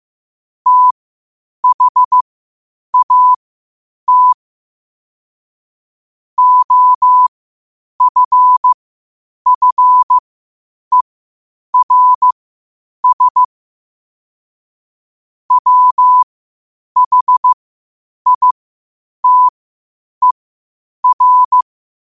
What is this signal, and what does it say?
Morse code, heard as a keyed tone, about 15 wpm, spaced as for 8 wpm, 1000 Hz. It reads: THAT OFFERS WHITER
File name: qotdfiles / 8wpm